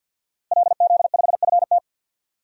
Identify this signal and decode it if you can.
Morse code, heard as a keyed tone, about 46 wpm, 700 Hz. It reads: LZ5FT
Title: DXCC-100 Call Signs - 5 Characters - Repeated 2x - 50wpm